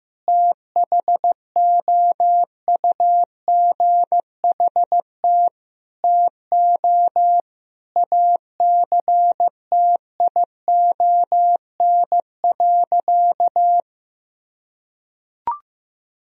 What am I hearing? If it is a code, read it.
Morse code, heard as a keyed tone, about 15 wpm, 700 Hz. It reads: THOUGHT TO ACTION.
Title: Sentences from Top 1000 Words - Review Part 4 - 15wpm